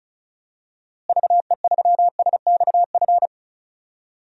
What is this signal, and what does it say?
Morse code, heard as a keyed tone, about 36 wpm, 700 Hz. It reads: VE3SXF
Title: DXCC-100 Call Signs - 6 Characters - Repeated 2x - 35wpm